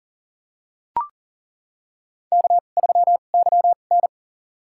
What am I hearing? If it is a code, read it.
Morse code, heard as a keyed tone, about 40 wpm, spaced as for 29 wpm, 700 Hz. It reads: K3YN
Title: USA Call Signs - 4 Characters - 2x Character Spacing - 40wpm